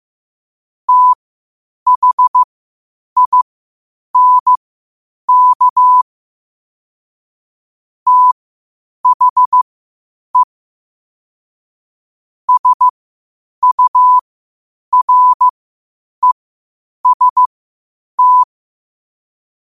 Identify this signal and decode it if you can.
Morse code, heard as a keyed tone, about 15 wpm, spaced as for 8 wpm, 1000 Hz. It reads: THINK THE SUREST
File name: qotdfiles / 8wpm